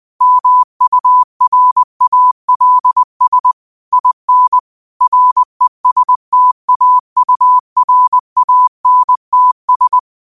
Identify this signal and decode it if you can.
Morse code, heard as a keyed tone, about 20 wpm, 1000 Hz. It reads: MURALS IN RESTAURANTS
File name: qotdfiles / 20wpm